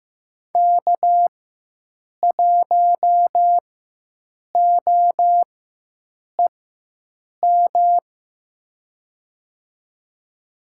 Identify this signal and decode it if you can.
Morse code, heard as a keyed tone, about 15 wpm, spaced as for 7 wpm, 700 Hz. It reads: K1OEM
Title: USA Call Signs - 4x Character Spacing - 15wpm